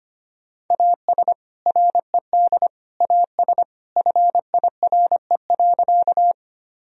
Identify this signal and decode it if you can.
Morse code, heard as a keyed tone, about 25 wpm, 700 Hz. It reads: AS RED AS FIRE.
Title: Sentences from Top 1000 Words - Review Part 2 - 25wpm